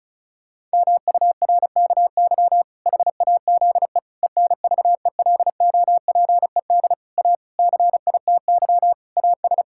Morse code, heard as a keyed tone, about 36 wpm, 700 Hz. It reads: MURKY HAZE ENVELOPED A CITY AS
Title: Pangram Challenge v2 - 35wpm